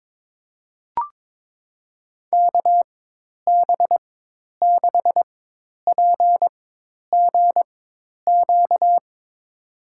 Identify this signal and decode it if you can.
Morse code, heard as a keyed tone, about 22 wpm, spaced as for 10 wpm, 700 Hz. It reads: KB6PGQ
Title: USA Call Signs - 4x Character Spacing - 22wpm